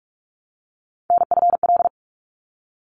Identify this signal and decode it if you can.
Morse code, heard as a keyed tone, about 44 wpm, 700 Hz. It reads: NFL
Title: ARRL Sections 45wpm